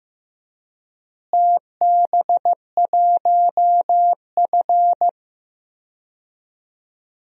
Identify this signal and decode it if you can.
Morse code, heard as a keyed tone, about 15 wpm, 700 Hz. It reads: TB1F
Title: DXCC-100 Call Signs - 4 Characters - Repeated 3x - 15wpm